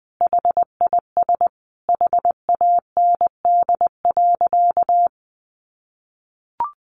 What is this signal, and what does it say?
Morse code, heard as a keyed tone, about 19 wpm, 700 Hz. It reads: HIS HAND.
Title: Sentences from Top 600 Words 20wpm